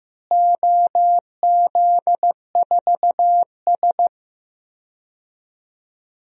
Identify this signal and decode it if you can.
Morse code, heard as a keyed tone, about 15 wpm, 700 Hz. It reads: OZ4S
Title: DXCC-100 Call Signs - 4 Characters - Repeated 3x - 15wpm